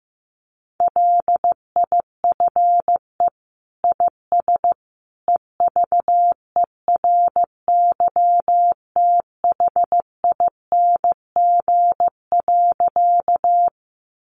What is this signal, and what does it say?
Morse code, heard as a keyed tone, about 15 wpm, 700 Hz. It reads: LIFE IS EVERYTHING.
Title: Sentences from Top 700 Words 15wpm